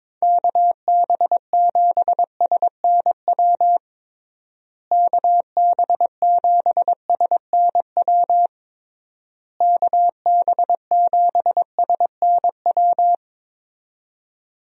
Morse code, heard as a keyed tone, about 22 wpm, 700 Hz. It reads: KB7SNW KB7SNW KB7SNW
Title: USA Call Signs - 6 Characters - Repeated 3x - 22wpm